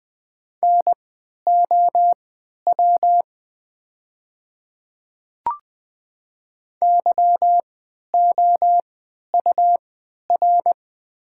Morse code, heard as a keyed tone, about 20 wpm, spaced as for 11 wpm, 700 Hz. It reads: NOW YOUR
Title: 3x Character Spacing - Top 100 Words - 20wpm